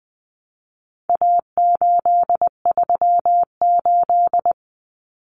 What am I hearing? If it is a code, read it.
Morse code, heard as a keyed tone, about 20 wpm, 700 Hz. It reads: A838
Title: Common Cut Numbers - 4-Digit 20wpm